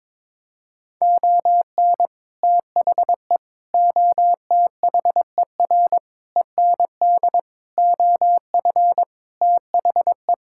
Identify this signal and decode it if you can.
Morse code, heard as a keyed tone, about 22 wpm, 700 Hz. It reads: ON THE OTHER END OF THE
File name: Sentences with 7-Word Phrases - 22wpm